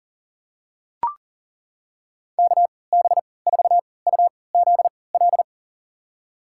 Morse code, heard as a keyed tone, about 37 wpm, spaced as for 22 wpm, 690 Hz. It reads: KB4UZL
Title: USA Call Signs - 3x Character Spacing - 40wpm